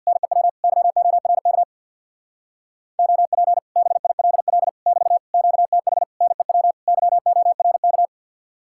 Morse code, heard as a keyed tone, about 49 wpm, 700 Hz. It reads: NEW YORK OP BILL <BT> QTH NEW YORK
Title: Speed-Racing Common QSO Phrases - 50wpm